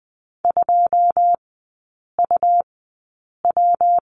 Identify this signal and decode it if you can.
Morse code, heard as a keyed tone, about 20 wpm, spaced as for 8 wpm, 690 Hz. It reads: 2UW
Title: 159 Lesson 26e - Letter g - Three Characters 20wpm